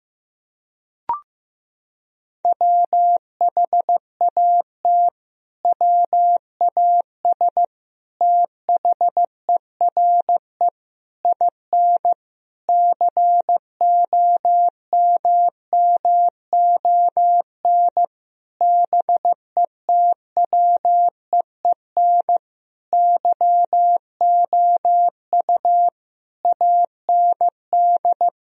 Morse code, heard as a keyed tone, about 15 wpm, 700 Hz. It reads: WHAT WAS THERE IN COMMON BETWEEN YOU AND
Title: Sentences from Top 1000 Words - Review Part 3 - 15wpm